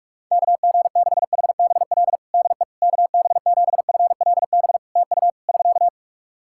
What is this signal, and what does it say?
Morse code, heard as a keyed tone, about 44 wpm, 700 Hz. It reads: KG6HBL DE KB7FLB TU <SK>
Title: Practice QSO - Exchange 4 - Ending - 45wpm